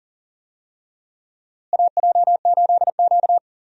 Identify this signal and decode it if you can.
Morse code, heard as a keyed tone, about 40 wpm, 700 Hz. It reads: AJ8Q
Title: USA Call Signs - 4 Characters - Repeated 3x - 40wpm